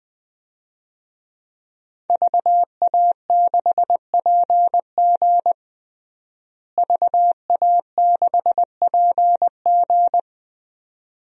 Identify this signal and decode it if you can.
Morse code, heard as a keyed tone, about 21 wpm, 700 Hz. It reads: VA6PG VA6PG
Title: DXCC-100 Call Signs - 5 Characters - Repeated 2x - 20wpm